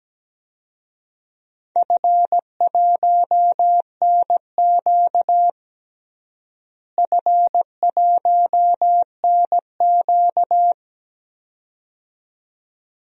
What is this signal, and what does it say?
Morse code, heard as a keyed tone, about 17 wpm, 700 Hz. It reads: F1NQ F1NQ
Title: DXCC-100 Call Signs - 4 Characters - Repeated 2x - 17wpm